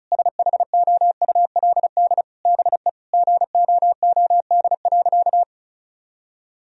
Morse code, heard as a keyed tone, about 36 wpm, 700 Hz. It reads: SHOULD BE GOOD.
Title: Sentences from Top 1000 Words - Review Part 1 - 35wpm